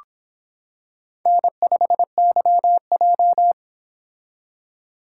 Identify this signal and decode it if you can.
Morse code, heard as a keyed tone, about 26 wpm, 700 Hz. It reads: N5YJ